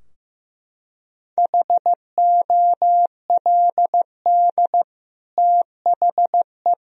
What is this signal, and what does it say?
Morse code, heard as a keyed tone, about 15 wpm, 700 Hz. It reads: HOLD THE
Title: Sentences from Top 1000 Words - Review Part 2 - 15wpm